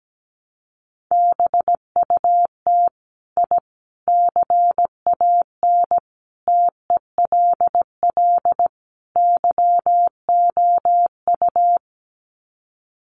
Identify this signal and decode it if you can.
Morse code, heard as a keyed tone, about 17 wpm, 700 Hz. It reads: BUT I CAN TELL YOU
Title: Sets of 5 Words Encore 17wpm